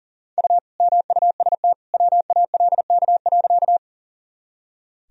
Morse code, heard as a keyed tone, about 40 wpm, 700 Hz. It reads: U MUST WALK.
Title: Sentences from Top 300 Words and Abbreviations 40wpm